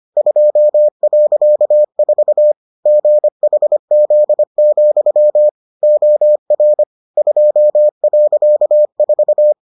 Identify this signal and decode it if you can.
Morse code, heard as a keyed tone, about 25 wpm, 600 Hz. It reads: 2.4 GHZ, OR 2.4